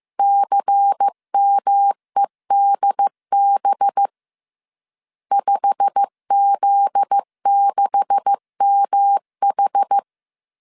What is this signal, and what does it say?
Morse code, heard as a keyed tone, about 15 wpm, 800 Hz. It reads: CMEDB 5Z6MH
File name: CW / DET / 1 9 1500C S C 14wpm 3245